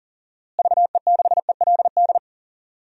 Morse code, heard as a keyed tone, about 40 wpm, 700 Hz. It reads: VE6ELD
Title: DXCC-100 Call Signs - 6 Characters - Repeated 3x - 40wpm